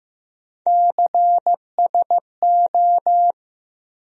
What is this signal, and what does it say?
Morse code, heard as a keyed tone, about 15 wpm, 700 Hz. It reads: CSO
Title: USA Call Sign Postfixes - 3 Letters 15wpm